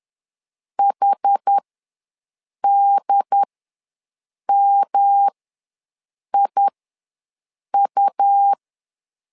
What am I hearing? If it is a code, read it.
Morse code, heard as a keyed tone, about 10 wpm, spaced as for 6 wpm, 800 Hz. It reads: HDMIU
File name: LEZIONI / 3 MWRO / 3 M ws 10wpm 2121